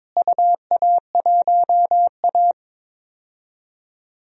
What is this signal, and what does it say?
Morse code, heard as a keyed tone, about 22 wpm, 700 Hz. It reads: UA1A
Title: DXCC-100 Call Signs - 4 Characters - Repeated 2x - 22wpm